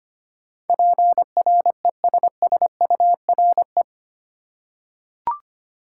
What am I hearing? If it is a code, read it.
Morse code, heard as a keyed tone, about 25 wpm, 700 Hz. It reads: PRESSURE
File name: Eight Letter Words - 25wpm